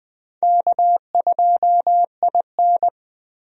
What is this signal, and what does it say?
Morse code, heard as a keyed tone, about 20 wpm, 700 Hz. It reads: K2IN
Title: USA Call Signs - 4 Characters - Repeated 2x - 20wpm